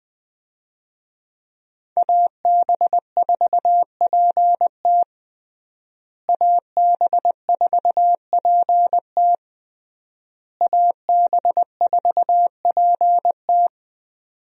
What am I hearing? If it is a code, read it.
Morse code, heard as a keyed tone, about 20 wpm, 700 Hz. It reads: AB4PT AB4PT AB4PT
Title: USA Call Signs - 5 Characters - Repeated 3x - 20wpm